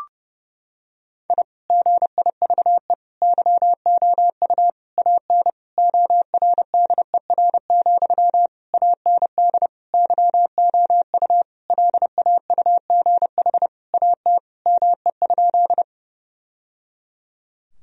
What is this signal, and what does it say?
Morse code, heard as a keyed tone, about 32 wpm, 700 Hz. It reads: I GIVE YOU AN ORDER, AND YOU LAUGH AT ME?
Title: Sentences from Top 1000 Words - Review Part 2 - 30wpm